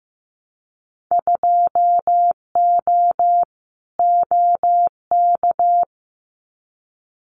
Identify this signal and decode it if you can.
Morse code, heard as a keyed tone, about 15 wpm, 700 Hz. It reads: 2O OK
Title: Winter Field Day Exchanges 15wpm